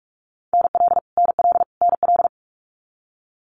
Morse code, heard as a keyed tone, about 46 wpm, 700 Hz. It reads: NL NL NL
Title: Canadian Provinces and Territories - Repeated 3x - 45wpm